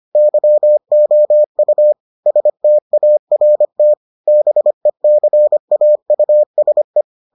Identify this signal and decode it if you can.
Morse code, heard as a keyed tone, about 25 wpm, 600 Hz. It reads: YOU START BECAUSE